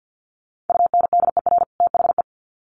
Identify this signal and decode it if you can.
Morse code, heard as a keyed tone, about 50 wpm, 700 Hz. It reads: UNDER THE
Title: Sets of 2 Words 50wpm